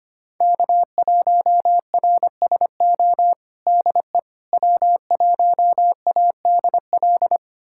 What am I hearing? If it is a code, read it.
Morse code, heard as a keyed tone, about 25 wpm, 700 Hz. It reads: K1RSO DE W1ADL
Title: USA Callsign DE Callsign - 5 Characters - 25wpm